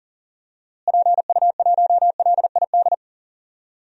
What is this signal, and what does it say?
Morse code, heard as a keyed tone, about 37 wpm, 700 Hz. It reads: PU1LID
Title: DXCC-100 Call Signs - 6 Characters - Repeated 3x - 40wpm